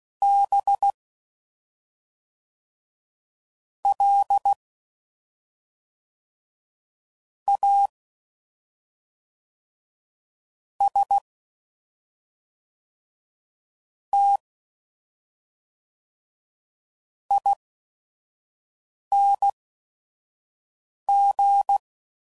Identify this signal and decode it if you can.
Morse code, heard as a keyed tone, about 16 wpm, spaced as for 5 wpm, 800 Hz. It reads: B L A S T ING
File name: Suffixes / Special ING's / SlowToFaster ING's